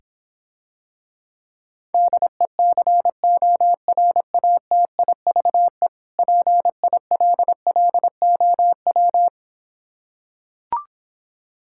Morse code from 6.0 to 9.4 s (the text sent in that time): PILLOW